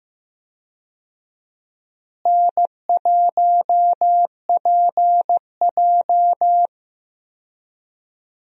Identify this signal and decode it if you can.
Morse code, heard as a keyed tone, about 15 wpm, 700 Hz. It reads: N1PJ
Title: USA Call Signs - 4 Characters - Repeated 2x - 15wpm